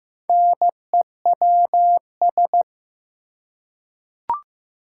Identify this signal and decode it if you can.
Morse code, heard as a keyed tone, about 15 wpm, 700 Hz. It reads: NEWS
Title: Four Letter Words - 15wpm